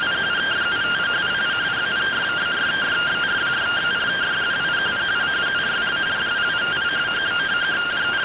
Two-tone teleprinter signal, about 170 Hz shift.